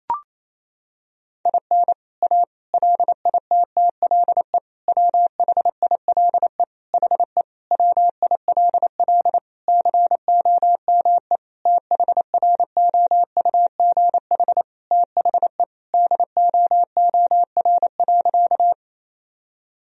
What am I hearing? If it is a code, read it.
Morse code, heard as a keyed tone, about 28 wpm, 700 Hz. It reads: IN A LITTLE WHILE HE WILL COME THROUGH THE DOOR.
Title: Sentences from Top 1000 Words - Review Part 1 - 28wpm